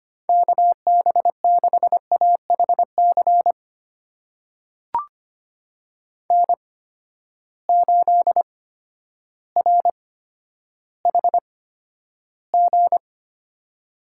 Morse code, heard as a keyed tone, about 25 wpm, spaced as for 11 wpm, 700 Hz. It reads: KB6AHC N8RHG
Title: USA Call Signs - 8x Character Spacing - 25wpm